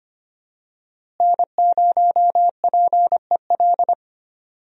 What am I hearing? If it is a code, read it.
Morse code, heard as a keyed tone, about 25 wpm, 700 Hz. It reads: N0PEL
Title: USA Call Signs - 5 Characters - Repeated 3x - 25wpm